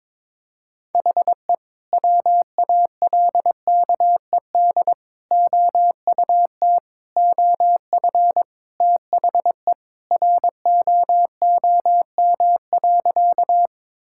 Morse code, heard as a keyed tone, about 22 wpm, 700 Hz. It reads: HE WALKED OUT OF THE ROOM.
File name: practice / 22wpm / Sentences with 2-Word Phrases - 22wpm